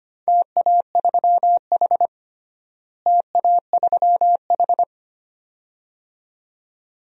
Morse code, heard as a keyed tone, about 25 wpm, 700 Hz. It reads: TA3H TA3H
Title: DXCC-100 Call Signs - 4 Characters - Repeated 2x - 25wpm